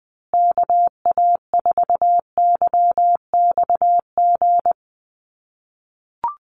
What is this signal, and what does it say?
Morse code, heard as a keyed tone, about 20 wpm, 700 Hz. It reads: KA4YXG
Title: USA Call Signs Repeated 2x 20wpm